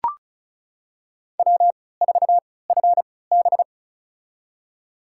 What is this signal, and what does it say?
Morse code, heard as a keyed tone, about 36 wpm, spaced as for 20 wpm, 700 Hz. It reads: W4FB
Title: USA Call Signs - 4 Characters - 3x Character Spacing - 35wpm